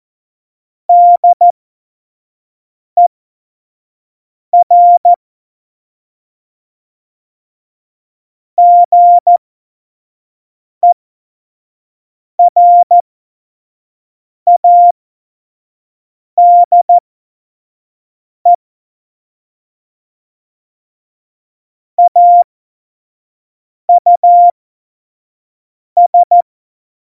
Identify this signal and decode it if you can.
Morse code, heard as a keyed tone, about 14 wpm, spaced as for 5 wpm, 700 Hz. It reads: DER GERADE AUS
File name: Morsekurs / Witze / Witz 11. Dezember 2018, 25 F70